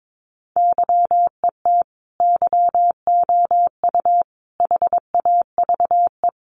Morse code, heard as a keyed tone, about 22 wpm, 700 Hz. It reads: YET YOU HAVE
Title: Sentences from Top 600 Words 22wpm